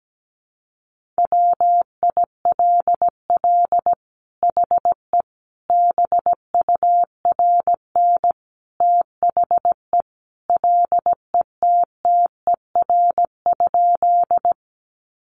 Morse code, heard as a keyed tone, about 17 wpm, 700 Hz. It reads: WILL HE BURN THE LETTER?
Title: Sentences from Top 1300 Words 17wpm